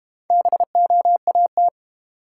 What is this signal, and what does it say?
Morse code, heard as a keyed tone, about 32 wpm, 700 Hz. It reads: BOAT